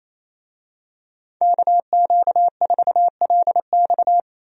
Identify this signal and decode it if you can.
Morse code, heard as a keyed tone, about 28 wpm, 700 Hz. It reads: KQ4LX
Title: USA Call Signs - 5 Characters - 28wpm